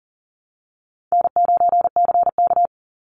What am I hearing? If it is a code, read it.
Morse code, heard as a keyed tone, about 40 wpm, 700 Hz. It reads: N9CK
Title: Easy Call Signs Repeated 3x - 40wpm